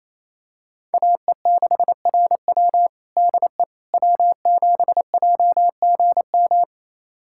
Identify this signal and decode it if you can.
Morse code, heard as a keyed tone, about 28 wpm, 700 Hz. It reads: AE6RW DE W7JGM
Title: USA Callsign DE Callsign - 5 Characters - 28wpm